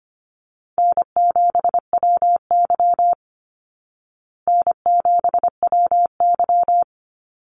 Morse code, heard as a keyed tone, about 25 wpm, 700 Hz. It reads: N7WY N7WY
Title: Call Signs Repeated 2x - 25wpm